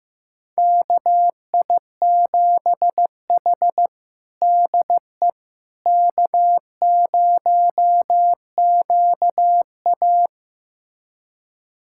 Morse code, heard as a keyed tone, about 15 wpm, 700 Hz. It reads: KI7H DE K0QA
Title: USA Callsign DE Callsign - 4 Characters - 15wpm